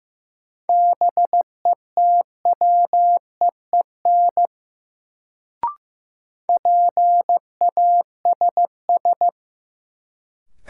Morse code, heard as a keyed tone, about 15 wpm, 700 Hz. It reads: BETWEEN PASS
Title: Sentences from Top 400 Words - New Word Drill - 15wpm